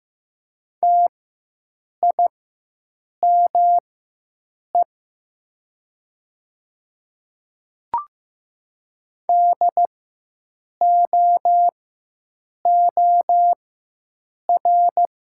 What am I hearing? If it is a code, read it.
Morse code, heard as a keyed tone, about 15 wpm, spaced as for 7 wpm, 700 Hz. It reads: TIME DOOR
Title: 4x Character Spacing - Top 300 Words - 15wpm